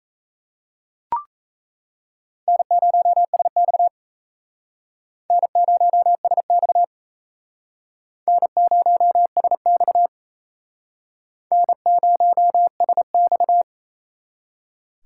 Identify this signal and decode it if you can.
Morse code, heard as a keyed tone, about 35 wpm, 700 Hz. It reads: N0SX N0SX N0SX N0SX
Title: Speed Racing - USA 1x2 Call Signs 28wpm